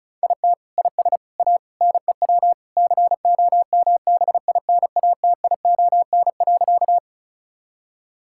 Morse code, heard as a keyed tone, about 36 wpm, 700 Hz. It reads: IT IS A NEW COMBINATION.
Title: Sentences from Top 1000 Words - Review Part 6 - 35wpm